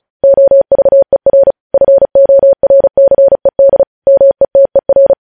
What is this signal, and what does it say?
Morse code, heard as a keyed tone, about 36 wpm, 550 Hz. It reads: OVER FORCED METER
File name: podcast / News headlines-1x-35x35